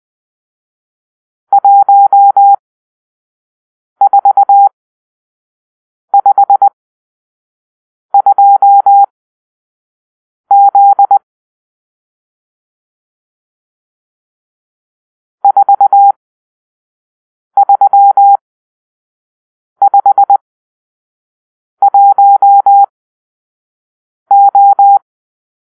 Morse code, heard as a keyed tone, about 20 wpm, spaced as for 5 wpm, 800 Hz. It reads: 1452Z 4351O
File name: arrl / Grp-6 Drill-3